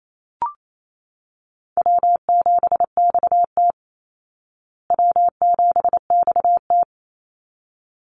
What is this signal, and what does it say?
Morse code, heard as a keyed tone, about 28 wpm, 700 Hz. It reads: W7XT W7XT
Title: Easy Call Signs Repeated 2x - 28wpm